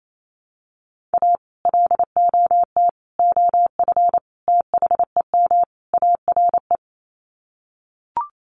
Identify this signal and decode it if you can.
Morse code, heard as a keyed tone, about 28 wpm, 700 Hz. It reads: A LOT OF THEM ARE
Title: Sets of 5 Words Encore2 28wpm